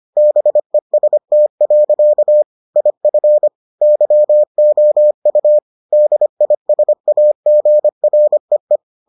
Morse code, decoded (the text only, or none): BEST. IF YOU DISAGREE